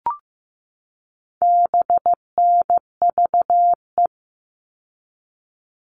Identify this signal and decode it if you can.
Morse code, heard as a keyed tone, about 15 wpm, 700 Hz. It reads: BNVE
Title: All Cut Numbers - 4-Digit 15wpm